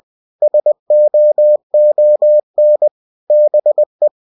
Morse code, heard as a keyed tone, about 20 wpm, 600 Hz. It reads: SOON BE
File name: cw-books / winnie-the-pooh / winnie-20wpm-page-0040